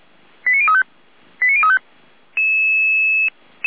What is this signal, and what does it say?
Five-tone selective call (ZVEI), heard as quick runs of five tones, tones of about 70 ms. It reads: 89035 89035
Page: SDR/ZVEI